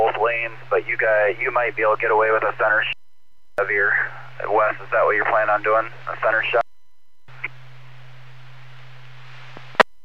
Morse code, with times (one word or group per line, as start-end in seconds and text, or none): none